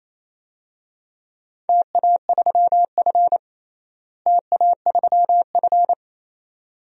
Morse code, heard as a keyed tone, about 28 wpm, 700 Hz. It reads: TA3F TA3F
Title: DXCC-100 Call Signs - 4 Characters - Repeated 2x - 28wpm